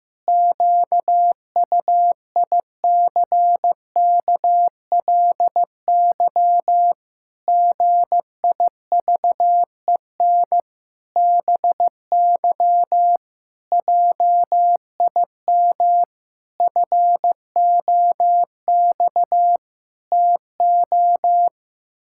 Morse code, heard as a keyed tone, about 15 wpm, 700 Hz. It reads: QUICKLY GIVEN BY JIM FOX TO
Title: Pangram Challenge v2 - 15wpm